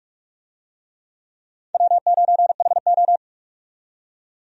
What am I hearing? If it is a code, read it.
Morse code, heard as a keyed tone, about 46 wpm, 700 Hz. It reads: W9HO
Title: USA Call Signs - 4 Characters - 45wpm